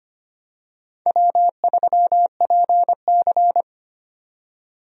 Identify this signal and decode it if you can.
Morse code, heard as a keyed tone, about 25 wpm, 700 Hz. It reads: W3PC
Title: USA Call Signs - 4 Characters - 25wpm